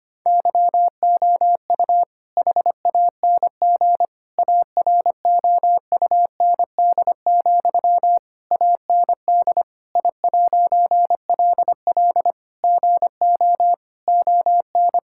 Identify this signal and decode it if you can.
Morse code, heard as a keyed tone, about 25 wpm, 700 Hz. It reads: YOU HANG AROUND, AND I'LL GO ON
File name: practice / 25wpm / All-Age Appropriate Jokes - 25wpm